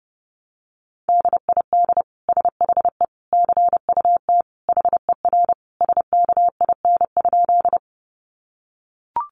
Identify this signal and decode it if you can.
Morse code, heard as a keyed tone, about 30 wpm, 700 Hz. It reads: DID SHE CUT HER SKIN?
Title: Sentences from Top 1300 Words 30wpm